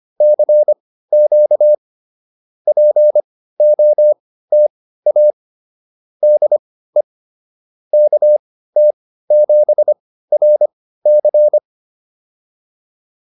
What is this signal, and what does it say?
Morse code, heard as a keyed tone, about 25 wpm, spaced as for 15 wpm, 600 Hz. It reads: CQ POTA DE KT7RC